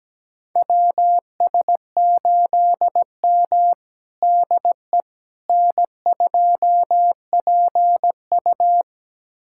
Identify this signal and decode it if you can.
Morse code, heard as a keyed tone, about 17 wpm, 700 Hz. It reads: WS8M DE N2PU
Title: USA Callsign DE Callsign - 4 Characters - 17wpm